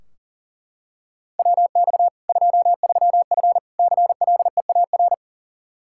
Morse code, heard as a keyed tone, about 40 wpm, 700 Hz. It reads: WX 23F CLEAR
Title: K1USN SST QSO Practice 40wpm